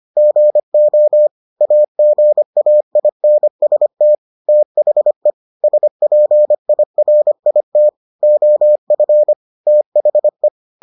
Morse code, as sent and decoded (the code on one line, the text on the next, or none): --. --- / .- --. .- .. -. ... - / - .... / ... .--. .. .-. .. - / --- ..-. / - .... .
GO AGAINST THE SPIRIT OF THE